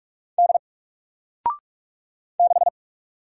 Morse code, heard as a keyed tone, about 45 wpm, 700 Hz. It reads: D 6
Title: Single Character (Letter:Number:Punctuation:Prosign) 45wpm